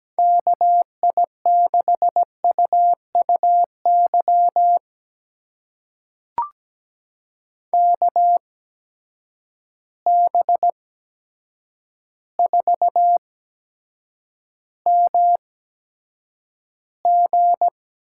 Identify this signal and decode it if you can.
Morse code, heard as a keyed tone, about 17 wpm, spaced as for 8 wpm, 700 Hz. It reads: KI6UUY KB4MG
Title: USA Call Signs - 6 Characters - 8x Character Spacing - 17wpm